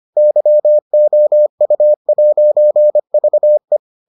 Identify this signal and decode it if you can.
Morse code, heard as a keyed tone, about 25 wpm, 600 Hz. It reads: YOU'VE